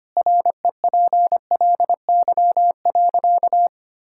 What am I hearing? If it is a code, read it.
Morse code, heard as a keyed tone, about 25 wpm, 700 Hz. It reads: REPLY.
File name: Easy Sentences with Hints - 25wpm